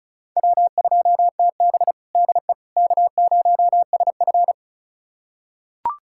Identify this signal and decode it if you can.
Morse code, heard as a keyed tone, about 36 wpm, 700 Hz. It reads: W2TB DE K0SF
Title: USA Callsign DE Callsign - 4 Characters - 35wpm